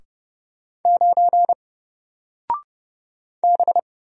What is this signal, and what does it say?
Morse code, heard as a keyed tone, about 30 wpm, 700 Hz. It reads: 9 B
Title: Single Character (Letter:Number:Punctuation:Prosign) Alt 30wpm